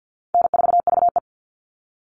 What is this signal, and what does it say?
Morse code, heard as a keyed tone, about 50 wpm, 700 Hz. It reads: N4VE
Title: All Cut Numbers - 4-Digit 50wpm